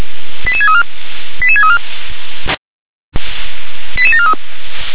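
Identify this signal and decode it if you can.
Five-tone selective call (ZVEI), heard as quick runs of five tones, tones of about 70 ms. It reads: 88634 88634 88743
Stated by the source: Five-tone ZVEI selective call received on SDR.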